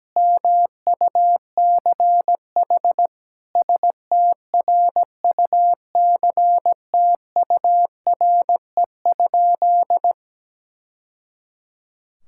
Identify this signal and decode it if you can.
Morse code, heard as a keyed tone, about 17 wpm, 700 Hz. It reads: MUCH STRUCTURE?
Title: Sentences from Top 1000 Words - Review Part 6 - 17wpm